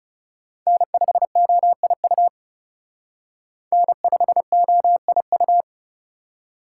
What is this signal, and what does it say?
Morse code, heard as a keyed tone, about 39 wpm, 700 Hz. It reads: N5OIU N5OIU